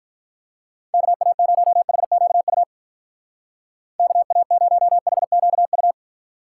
Morse code, heard as a keyed tone, about 45 wpm, 700 Hz. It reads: KA0HQU KA0HQU